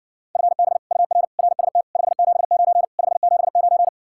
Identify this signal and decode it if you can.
Morse code, heard as a keyed tone, about 61 wpm, 700 Hz. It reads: FB UR RST 579 579